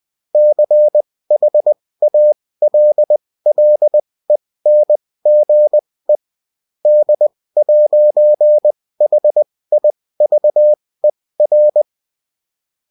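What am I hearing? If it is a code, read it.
Morse code, heard as a keyed tone, about 20 wpm, spaced as for 16 wpm, 600 Hz. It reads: CHALLENGE D'HIVER
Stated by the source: Morse code bulletin of a French amateur CW union, sent as text and in CW.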